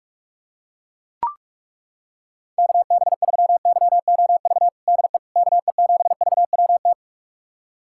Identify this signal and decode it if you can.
Morse code, heard as a keyed tone, about 41 wpm, 700 Hz. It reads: KB3YOV DE KE7VWT
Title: USA Callsign DE Callsign - 6 Characters - 45wpm